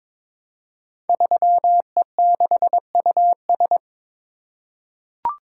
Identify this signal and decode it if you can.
Morse code, heard as a keyed tone, about 22 wpm, 700 Hz. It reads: 3E6US
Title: Code Groups - 5 Letter:Numbers - 22wpm